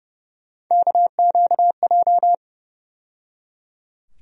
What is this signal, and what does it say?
Morse code, heard as a keyed tone, about 30 wpm, 700 Hz. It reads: KQJ
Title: USA Call Sign Postfixes - 3 Letters 30wpm